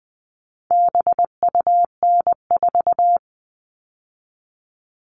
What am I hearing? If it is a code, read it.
Morse code, heard as a keyed tone, about 20 wpm, 700 Hz. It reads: BUN4
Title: All Cut Numbers - 4-Digit 20wpm